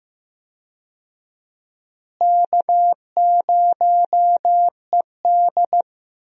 Morse code, heard as a keyed tone, about 15 wpm, 700 Hz. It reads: K0ED
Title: USA Call Signs - 4 Characters - Repeated 3x - 15wpm